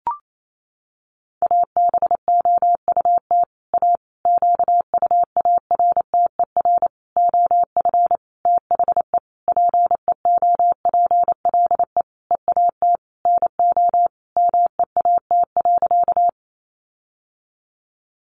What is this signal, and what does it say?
Morse code, heard as a keyed tone, about 28 wpm, 700 Hz. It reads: ABOUT A QUARTER OF THE PEOPLE EAT NO MEAT.
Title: Sentences from Top 1300 Words 28wpm